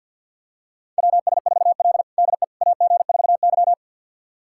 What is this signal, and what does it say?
Morse code, heard as a keyed tone, about 46 wpm, 700 Hz. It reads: WS4L DE AG4Y
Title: USA Callsign DE Callsign - 4 Characters - 50wpm